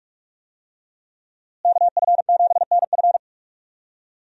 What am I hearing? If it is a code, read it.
Morse code, heard as a keyed tone, about 42 wpm, 700 Hz. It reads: KF7NF
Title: USA Call Signs - 5 Characters - Repeated 3x - 45wpm